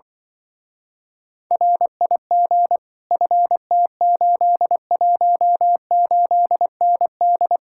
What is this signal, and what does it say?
Morse code, heard as a keyed tone, about 24 wpm, 700 Hz. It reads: RIG FT818ND
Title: Speed-Racing Common QSO Phrases Encore - 20wpm